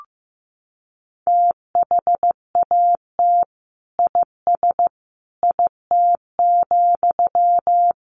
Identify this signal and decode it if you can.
Morse code, heard as a keyed tone, about 15 wpm, 700 Hz. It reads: THAT IS IT,
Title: Sentences from Top 800 Words 15wpm